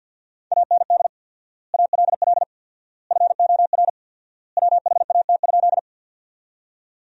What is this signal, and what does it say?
Morse code, heard as a keyed tone, about 53 wpm, 700 Hz. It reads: AND ALL FOR WHAT?
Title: 4x Word-Spacing - Sentences from Top 100 Words - 50wpm